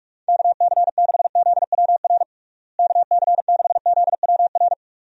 Morse code, heard as a keyed tone, about 46 wpm, 700 Hz. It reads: KC6ZWR KC6ZWR
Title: USA Call Signs - 6 Characters - Repeated 3x - 45wpm